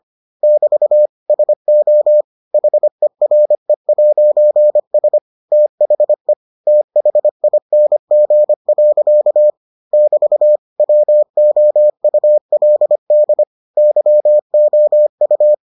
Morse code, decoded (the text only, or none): <BT> SO HERE'S THE THING. <BT> WOULD YOU